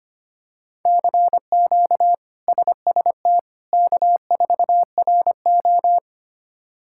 Morse code, heard as a keyed tone, about 25 wpm, 700 Hz. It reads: CQ SST K4RO
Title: K1USN SST QSO Practice - 25wpm